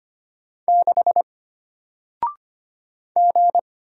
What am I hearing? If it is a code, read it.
Morse code, heard as a keyed tone, about 25 wpm, 700 Hz. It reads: 6 G
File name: Single-Letter-Number-25wpm